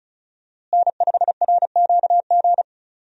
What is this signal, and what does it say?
Morse code, heard as a keyed tone, about 34 wpm, 700 Hz. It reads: N5RQG
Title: USA Call Signs - 5 Characters - Repeated 3x - 35wpm